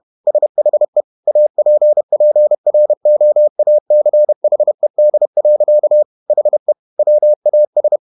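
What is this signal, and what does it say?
Morse code, heard as a keyed tone, about 31 wpm, 600 Hz. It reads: SHE APPROACHED. HE WAS
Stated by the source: Morse code at 30 wpm, the novel Never by Ken Follett in English, from CW med Gnister.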